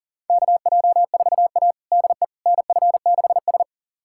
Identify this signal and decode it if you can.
Morse code, heard as a keyed tone, about 40 wpm, 700 Hz. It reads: KJ4A DE NF6S
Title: USA Callsign DE Callsign - 4 Characters - 40wpm